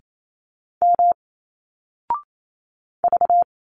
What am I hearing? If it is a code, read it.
Morse code, heard as a keyed tone, about 28 wpm, 700 Hz. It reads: M V
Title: Single Letters 28wpm